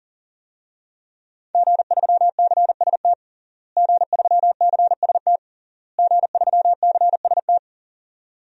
Morse code, heard as a keyed tone, about 37 wpm, 700 Hz. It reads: G3CST G3CST G3CST
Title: DXCC-100 Call Signs - 5 Characters - Repeated 3x - 40wpm